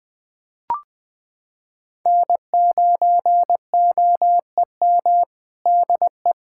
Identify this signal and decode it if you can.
Morse code, heard as a keyed tone, about 20 wpm, 700 Hz. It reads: N9OEM DE
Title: USA Callsign DE Callsign - 5 Characters - 20wpm